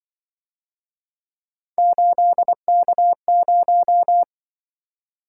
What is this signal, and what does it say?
Morse code, heard as a keyed tone, about 25 wpm, 700 Hz. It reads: 8K0
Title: Speed-Racing - Three Letter-Number Combinations - 20wpm